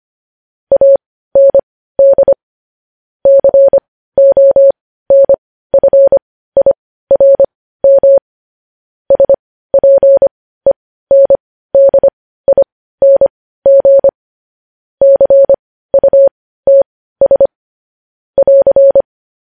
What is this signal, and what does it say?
Morse code, heard as a keyed tone, about 25 wpm, spaced as for 15 wpm, 550 Hz. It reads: AND CONFIRM SPENDING CUTS <AR>
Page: news-headlines-in-morse-code-at-15-wpm